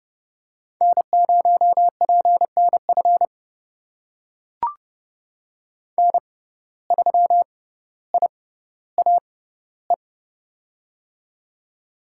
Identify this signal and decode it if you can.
Morse code, heard as a keyed tone, about 32 wpm, spaced as for 16 wpm, 700 Hz. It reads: N0PNF N3IAE
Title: USA Call Signs - 5 Characters - 6x Character Spacing - 30wpm